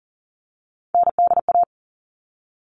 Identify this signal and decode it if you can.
Morse code, heard as a keyed tone, about 40 wpm, 700 Hz. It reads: NDA
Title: All Cut Numbers - 3-Digit 40wpm